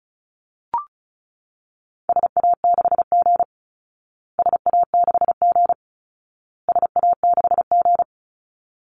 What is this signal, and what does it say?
Morse code, heard as a keyed tone, about 34 wpm, 700 Hz. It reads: SA6G SA6G SA6G
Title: Call Signs Repeated 2x - 35wpm